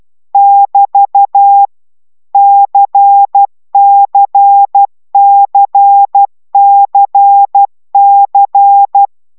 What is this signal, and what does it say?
Morse code, heard as a keyed tone, about 12 wpm, 800 Hz. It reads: <BT> CCCCC